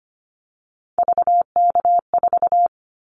Morse code, heard as a keyed tone, about 25 wpm, 700 Hz. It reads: VK4
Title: Callsign Prefixes for Top 100 DXCC - 25wpm